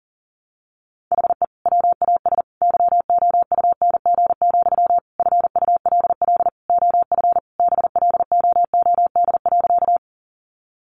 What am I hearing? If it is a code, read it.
Morse code, heard as a keyed tone, about 40 wpm, 700 Hz. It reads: HE WAS YOUNG, FULL OF BLOOD.
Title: Sentences from Top 1100 Words 40wpm